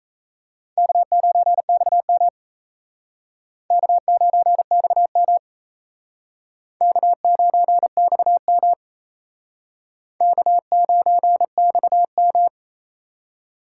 Morse code, heard as a keyed tone, about 36 wpm, 700 Hz. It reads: K9XM K9XM K9XM K9XM